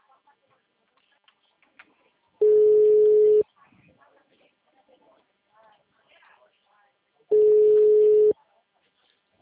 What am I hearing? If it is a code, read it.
Ringback tone (Europe, 425 Hz), with speech.
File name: out- -2002- - - .83682